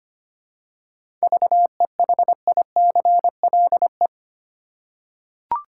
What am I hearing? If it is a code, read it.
Morse code, heard as a keyed tone, about 25 wpm, 700 Hz. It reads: VEHICLE
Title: Seven Letter Words - 25wpm